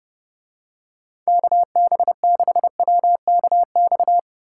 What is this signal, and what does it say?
Morse code, heard as a keyed tone, about 30 wpm, 700 Hz. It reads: KB6WKX